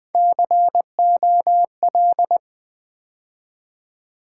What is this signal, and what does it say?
Morse code, heard as a keyed tone, about 20 wpm, 700 Hz. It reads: COL
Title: Build A Word - 3 Letters At A Time - 9-Letter Words - 20wpm